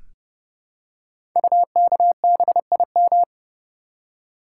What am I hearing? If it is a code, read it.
Morse code, heard as a keyed tone, about 30 wpm, 700 Hz. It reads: UKBIM
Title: Code Groups - 5 Letters - 30wpm